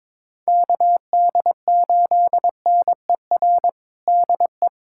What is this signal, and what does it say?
Morse code, heard as a keyed tone, about 22 wpm, 700 Hz. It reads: KD8NER DE
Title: USA Callsign DE Callsign - 6 Characters - 22wpm